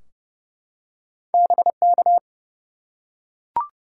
Morse code, heard as a keyed tone, about 30 wpm, 700 Hz. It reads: BK
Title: Top 180 QSO Elements 30wpm